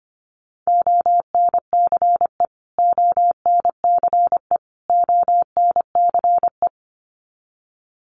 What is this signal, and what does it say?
Morse code, heard as a keyed tone, about 25 wpm, 700 Hz. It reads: ONCE ONCE ONCE
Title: Top 500 Words Repeated 25wpm